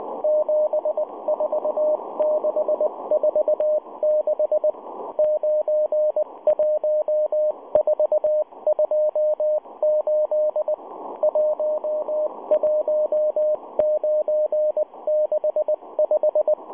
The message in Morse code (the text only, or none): 74646 91428 11965